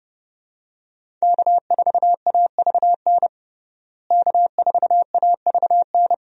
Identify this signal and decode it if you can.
Morse code, heard as a keyed tone, about 31 wpm, 700 Hz. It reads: K4AVN K4AVN
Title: USA Call Signs - 5 Characters - Repeated 3x - 30wpm